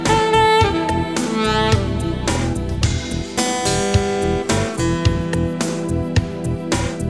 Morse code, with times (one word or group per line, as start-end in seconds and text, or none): none